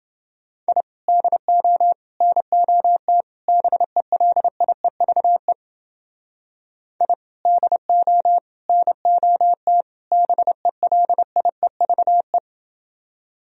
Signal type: Morse code

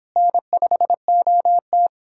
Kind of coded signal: Morse code

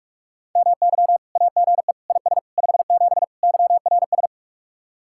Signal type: Morse code

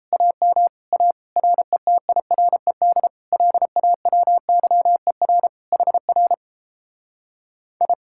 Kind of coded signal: Morse code

